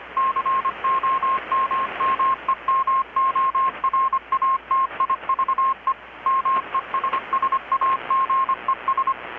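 Morse code, heard as a keyed tone, about 25 wpm, 1070 Hz, with radio noise. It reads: COMMEMORATIVE MESSAGES
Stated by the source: Morse from KPH's Night of Nights XX maritime radiotelegraphy event, received in Chiapas, Mexico about 3,800 km away on a Kenwood R-600 with a grounded 10 m long-wire antenna.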